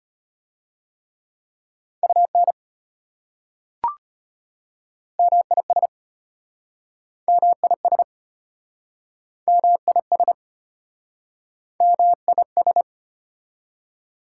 Morse code, heard as a keyed tone, about 33 wpm, 690 Hz. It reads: UN MIS MIS MIS MIS